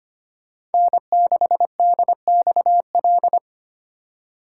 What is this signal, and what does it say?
Morse code, heard as a keyed tone, about 25 wpm, 700 Hz. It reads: N6DXL
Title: USA Call Signs - 5 Characters - 4x Character Spacing - 25wpm